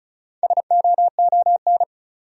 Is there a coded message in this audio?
Yes: Morse code from 0.4 to 1.8 s, heard as a keyed tone, about 46 wpm, 700 Hz. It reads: SOON